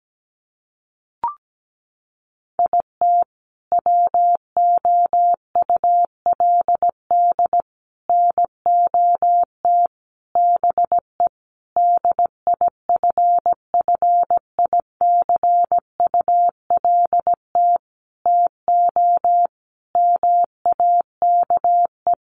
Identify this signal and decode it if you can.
Morse code, heard as a keyed tone, about 17 wpm, 700 Hz. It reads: IT WOULD NOT BE DIFFICULT TO MAKE
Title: Sentences from Top 600 Words 17wpm